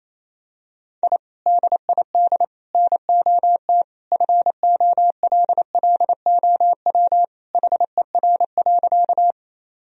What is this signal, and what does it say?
Morse code, heard as a keyed tone, about 27 wpm, 700 Hz. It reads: I DID NOT FOLLOW HER.
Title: Sentences from Top 1000 Words - Review Part 1 - 28wpm